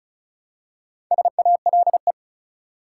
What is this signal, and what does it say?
Morse code, heard as a keyed tone, about 36 wpm, 700 Hz. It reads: SALE
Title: Four Letter Words - 35wpm